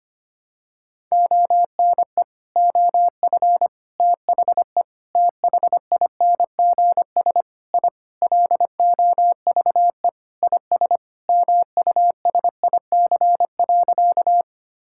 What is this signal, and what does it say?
Morse code, heard as a keyed tone, about 25 wpm, 700 Hz. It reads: ONE OF THE THINGS I LOVE IS MUSIC.
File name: practice / 25wpm / Sentences with 4-Word Phrases - 25wpm